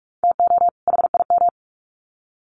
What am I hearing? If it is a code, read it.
Morse code, heard as a keyed tone, about 44 wpm, 700 Hz. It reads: TO HIM
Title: Sets of 2 Words 45wpm